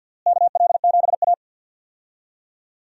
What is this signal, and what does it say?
Morse code, heard as a keyed tone, about 52 wpm, 700 Hz. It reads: KL7A
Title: DXCC-100 Call Signs - 4 Characters - Repeated 3x - 50wpm